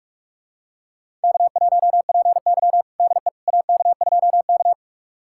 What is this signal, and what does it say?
Morse code, heard as a keyed tone, about 46 wpm, 700 Hz. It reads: K1PY DE AK2K
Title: USA Callsign DE Callsign - 4 Characters - 45wpm